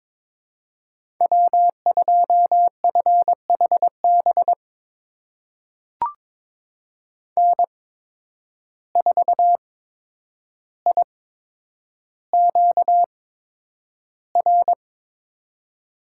Morse code, heard as a keyed tone, about 21 wpm, spaced as for 9 wpm, 700 Hz. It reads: W2FHB N4IQR
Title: USA Call Signs - 5 Characters - 8x Character Spacing - 22wpm